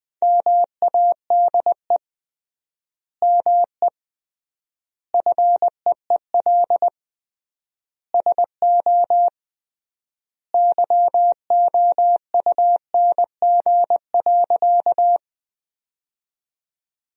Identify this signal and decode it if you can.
Morse code, heard as a keyed tone, about 20 wpm, 700 Hz. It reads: MADE ME FEEL SO YOUNG.
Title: 3x Word-Spacing - Sentences from Top 400 Words - 20wpm